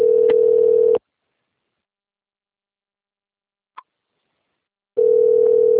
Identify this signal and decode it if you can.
Ringback tone (North America), 440 + 480 Hz.